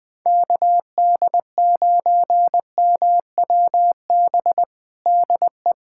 Morse code, heard as a keyed tone, about 21 wpm, 700 Hz. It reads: KD9MWB DE
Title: USA Callsign DE Callsign - 6 Characters - 20wpm